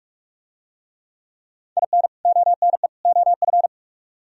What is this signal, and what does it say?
Morse code, heard as a keyed tone, about 46 wpm, 700 Hz. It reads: IN ONE OF